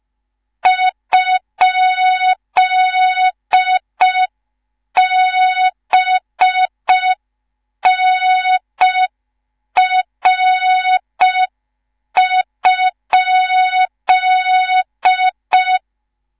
Morse code, heard as a keyed tone, about 5 wpm, 3700 Hz. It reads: ?BNR?
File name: Morze-HA2MN / hang / 19 GYAKORLAT